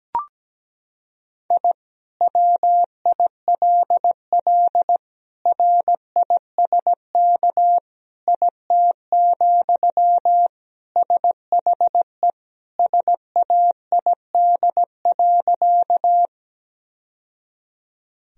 Morse code, heard as a keyed tone, about 17 wpm, 700 Hz. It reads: I WILL RISK IT, SHE SAID.
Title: Sentences from Top 1000 Words - Review Part 4 - 17wpm